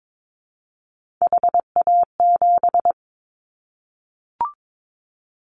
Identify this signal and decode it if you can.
Morse code, heard as a keyed tone, about 22 wpm, 700 Hz. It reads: HA7
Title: Callsign Prefixes for Top 25 DXCC - 22wpm